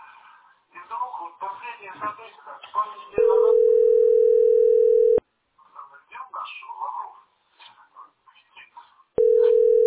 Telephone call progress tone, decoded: ringback tone

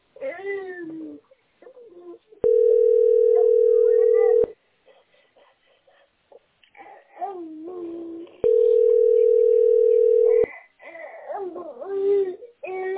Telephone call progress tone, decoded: ringback tone

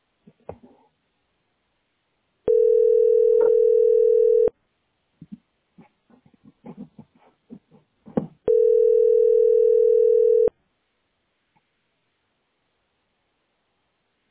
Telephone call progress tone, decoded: ringback tone

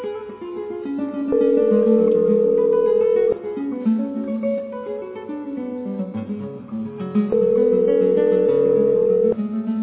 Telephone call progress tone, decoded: ringback tone

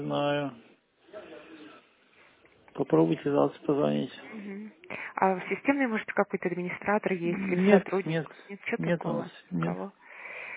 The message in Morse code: none

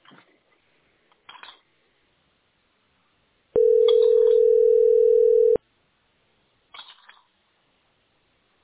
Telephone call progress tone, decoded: ringback tone

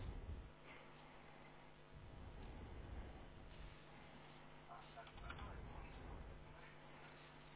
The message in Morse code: none

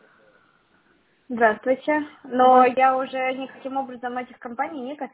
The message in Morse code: none

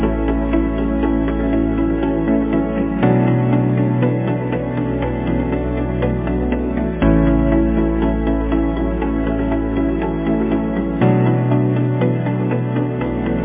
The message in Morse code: none

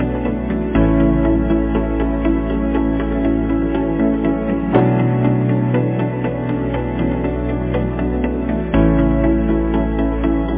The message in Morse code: none

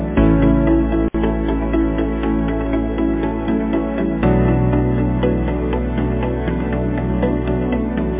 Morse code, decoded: none

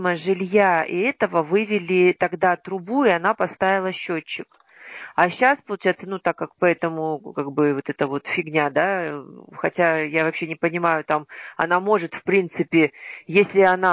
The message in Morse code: none

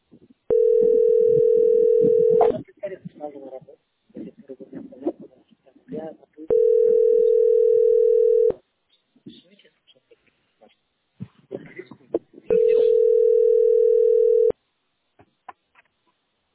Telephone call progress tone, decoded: ringback tone

